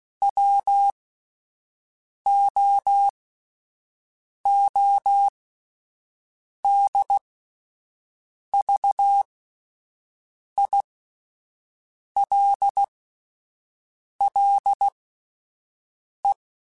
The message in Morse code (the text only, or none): WOODVILLE